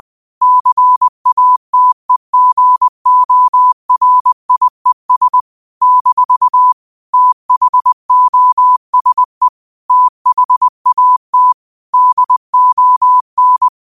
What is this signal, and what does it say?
Morse code, heard as a keyed tone, about 20 wpm, 1000 Hz. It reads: CATEGORIES - THOSE THAT DON
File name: qotdfiles / 20wpm